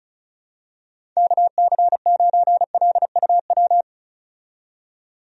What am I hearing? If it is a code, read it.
Morse code, heard as a keyed tone, about 34 wpm, 700 Hz. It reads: KC9LUW
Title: USA Call Signs - 6 Characters - 35wpm